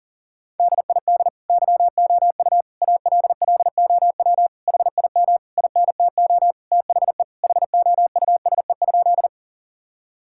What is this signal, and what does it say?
Morse code, heard as a keyed tone, about 40 wpm, 700 Hz. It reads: DID YOU ALLOW HIM INTO THE HOUSE?
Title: Sentences from Top 1000 Words - Review Part 5 - 40wpm